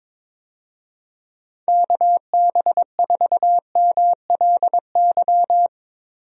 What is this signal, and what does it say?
Morse code, heard as a keyed tone, about 22 wpm, 700 Hz. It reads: KB4MLY
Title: USA Call Signs - 6 Characters - Repeated 2x - 22wpm